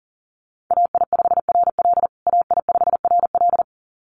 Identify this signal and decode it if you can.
Morse code, heard as a keyed tone, about 40 wpm, 700 Hz. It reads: AI5RL AI5RL
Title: USA Call Signs Repeated 2x 40wpm